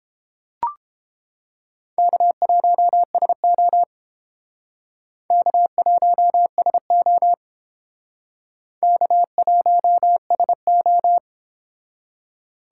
Morse code, heard as a keyed tone, about 29 wpm, 700 Hz. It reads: K1SO K1SO K1SO